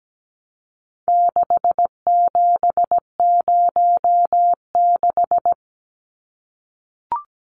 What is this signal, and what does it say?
Morse code, heard as a keyed tone, about 17 wpm, 700 Hz. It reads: 6706
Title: Numbers - 4-Digit 17wpm